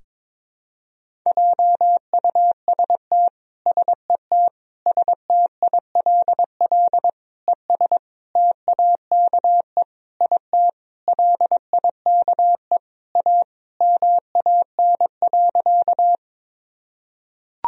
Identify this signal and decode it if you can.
Morse code, heard as a keyed tone, about 22 wpm, 700 Hz. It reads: JUST SET STILL ES TAKE IT LIKE A MAN.
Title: Sentences from Top 300 Words and Abbreviations 22wpm